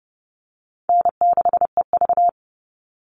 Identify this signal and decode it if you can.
Morse code, heard as a keyed tone, about 28 wpm, 700 Hz. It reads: N6EV
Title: Easy Call Signs Repeated 3x - 30wpm